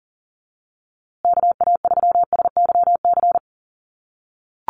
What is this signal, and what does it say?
Morse code, heard as a keyed tone, about 40 wpm, 700 Hz. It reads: KA3SYC
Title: USA Call Signs Repeated 2x 40wpm